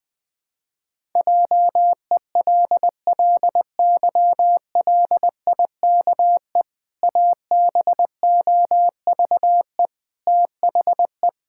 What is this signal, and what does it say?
Morse code, heard as a keyed tone, about 20 wpm, 700 Hz. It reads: JELLYLIKE ABOVE THE
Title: Pangram Challenge v2 - 20wpm